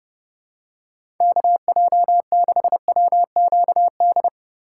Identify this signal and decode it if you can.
Morse code, heard as a keyed tone, about 30 wpm, 700 Hz. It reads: KJ6WQD
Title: USA Call Signs - 6 Characters - 2x Character Spacing - 30wpm